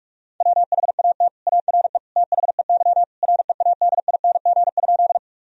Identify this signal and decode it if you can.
Morse code, heard as a keyed tone, about 46 wpm, 700 Hz. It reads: WHAT ARE THEY READING?
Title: Sentences from Top 1000 Words - Review Part 3 - 45wpm